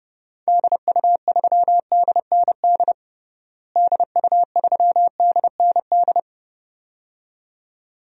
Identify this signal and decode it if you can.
Morse code, heard as a keyed tone, about 30 wpm, 700 Hz. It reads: DU3DND DU3DND
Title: DXCC-100 Call Signs - 6 Characters - Repeated 2x - 30wpm